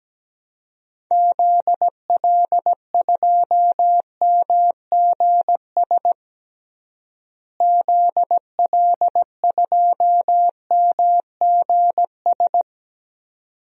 Morse code, heard as a keyed tone, about 17 wpm, 700 Hz. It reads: ZL2MGS ZL2MGS
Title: DXCC-100 Call Signs - 6 Characters - Repeated 2x - 17wpm